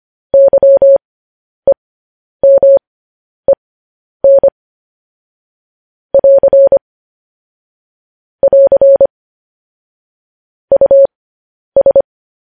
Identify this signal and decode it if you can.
Morse code, heard as a keyed tone, about 25 wpm, spaced as for 10 wpm, 550 Hz. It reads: YEMEN <AR> <AR> US